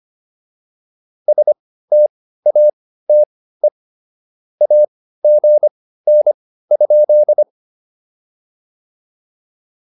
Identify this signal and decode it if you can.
Morse code, heard as a keyed tone, about 25 wpm, spaced as for 15 wpm, 590 Hz. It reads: STATE AGN?